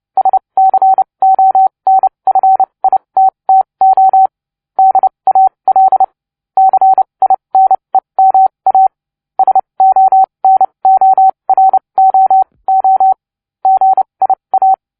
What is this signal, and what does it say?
Morse code, heard as a keyed tone, about 29 wpm, spaced as for 23 wpm, 760 Hz. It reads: SCONFITTO DAL CINEMA SONOROO GIA